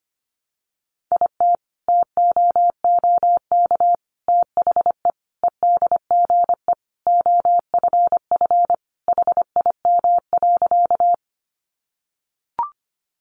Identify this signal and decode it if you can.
Morse code, heard as a keyed tone, about 25 wpm, 700 Hz. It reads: IT TOOK THE EDGE OFF HIM.
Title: Sentences from Top 1300 Words 25wpm